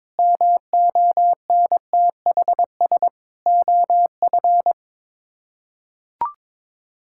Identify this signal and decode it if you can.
Morse code, heard as a keyed tone, about 22 wpm, 700 Hz. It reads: MONTHS OF
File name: practice / 22wpm / Sets of 6 Words - 22wpm